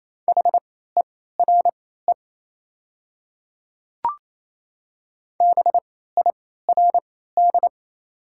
Morse code, heard as a keyed tone, about 28 wpm, spaced as for 16 wpm, 700 Hz. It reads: HERE BIRD
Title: 3x Character Spacing - Top 500 Words - 28wpm